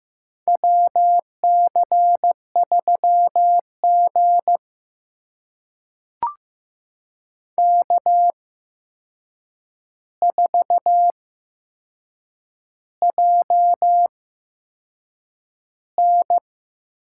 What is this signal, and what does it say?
Morse code, heard as a keyed tone, about 15 wpm, spaced as for 6 wpm, 700 Hz. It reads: WC3G K4JN
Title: USA Call Signs - 8x Character Spacing - 15wpm